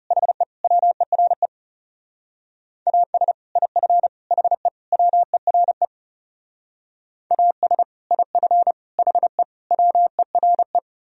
Morse code, heard as a keyed tone, about 39 wpm, 700 Hz. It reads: HE WERE AS IF HE WERE AS IF HE WERE